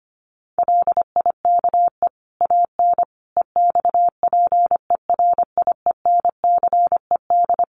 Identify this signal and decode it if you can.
Morse code, heard as a keyed tone, about 25 wpm, 700 Hz. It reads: LIKE AN EXPERIENCED
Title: Sentences from Top 1500 Words 25wpm